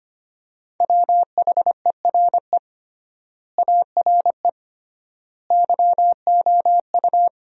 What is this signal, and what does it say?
Morse code, heard as a keyed tone, about 25 wpm, 700 Hz. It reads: WHERE ARE YOU
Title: 3x Word-Spacing - Sentences from Top 200 Words - 25wpm